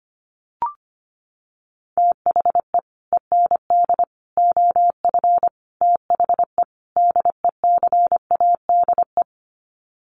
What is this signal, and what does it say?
Morse code, heard as a keyed tone, about 25 wpm, 700 Hz. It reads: THE END OF THE DECADE
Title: Sets of 5 Words Encore 25wpm